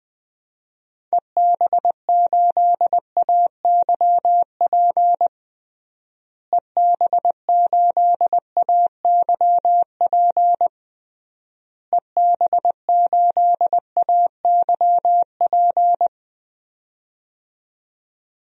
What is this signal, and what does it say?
Morse code, heard as a keyed tone, about 21 wpm, 700 Hz. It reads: EB8AYP EB8AYP EB8AYP
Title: DXCC-100 Call Signs - 6 Characters - Repeated 3x - 20wpm